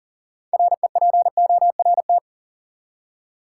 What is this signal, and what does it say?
Morse code, heard as a keyed tone, about 40 wpm, 700 Hz. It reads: REPORT
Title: Six Letter Words - 40wpm